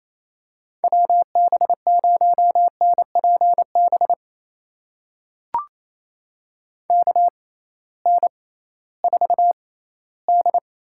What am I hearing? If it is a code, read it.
Morse code, heard as a keyed tone, about 28 wpm, spaced as for 16 wpm, 700 Hz. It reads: WB0NPB KN4D
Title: USA Call Signs - 6 Characters - 6x Character Spacing - 28wpm